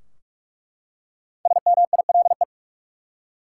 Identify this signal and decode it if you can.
Morse code, heard as a keyed tone, about 46 wpm, 700 Hz. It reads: SMILE
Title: Five Letter Words - 45wpm